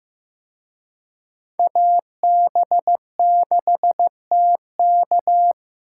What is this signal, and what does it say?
Morse code, heard as a keyed tone, about 18 wpm, 700 Hz. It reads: AB6TK
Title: Speed Racing - USA 2x2 Call Signs 15wpm